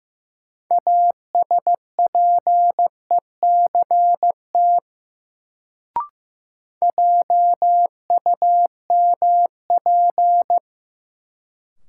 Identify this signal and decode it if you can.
Morse code, heard as a keyed tone, about 16 wpm, 700 Hz. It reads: ASPECT JUMP
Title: Sentences from Top 1300 Words - New Word Drill Alt - 15wpm